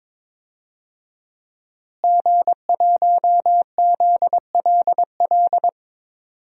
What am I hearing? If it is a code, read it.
Morse code, heard as a keyed tone, about 22 wpm, 700 Hz. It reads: G1ZLL
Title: DXCC-100 Call Signs - 5 Characters - Repeated 3x - 22wpm